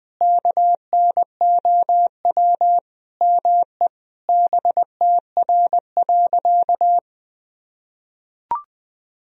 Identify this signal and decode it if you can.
Morse code, heard as a keyed tone, about 20 wpm, 700 Hz. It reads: KNOW ME BTR.
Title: Sentences from Top 400 Words and Abbreviations 20wpm